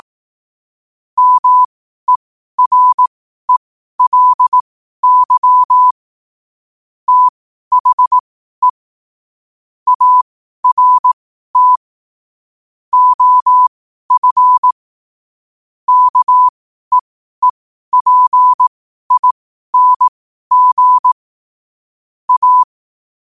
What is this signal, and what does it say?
Morse code, heard as a keyed tone, about 18 wpm, spaced as for 12 wpm, 1000 Hz. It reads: MERELY THE ART OF KEEPING A
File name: qotdfiles / 12wpm